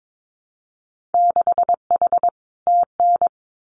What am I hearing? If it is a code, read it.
Morse code, heard as a keyed tone, about 22 wpm, 700 Hz. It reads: 6H TN
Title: Winter Field Day Exchanges Encore 22wpm